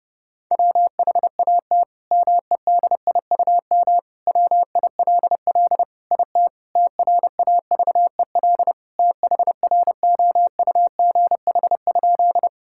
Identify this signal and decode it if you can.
Morse code, heard as a keyed tone, about 30 wpm, 700 Hz. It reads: WHAT MEDIUM WILL IT TRAVEL THROUGH?
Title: Sentences from Top 1000 Words - Review Part 6 - 30wpm